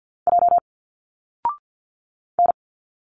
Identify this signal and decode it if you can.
Morse code, heard as a keyed tone, about 51 wpm, 700 Hz. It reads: J N
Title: Single Letters 50wpm